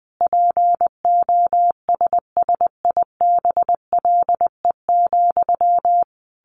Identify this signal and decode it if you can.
Morse code, heard as a keyed tone, about 20 wpm, 700 Hz. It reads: POSSIBLE,
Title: Sentences from Top 800 Words 20wpm